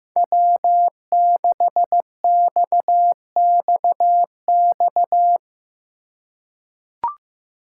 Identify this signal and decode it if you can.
Morse code, heard as a keyed tone, about 15 wpm, 700 Hz. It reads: W6XXX
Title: USA Call Signs - 5 Characters - 3x Character Spacing - 15wpm